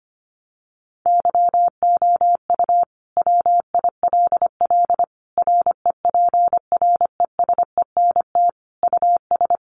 Morse code, heard as a keyed tone, about 25 wpm, 700 Hz. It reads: YOU WILL REPRESENT US
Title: Sentences from Top 1200 Words 25wpm